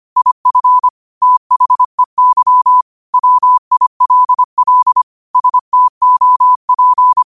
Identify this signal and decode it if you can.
Morse code, heard as a keyed tone, about 25 wpm, 1000 Hz. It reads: IF THEY WILL STOP
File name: qotdfiles / 25wpm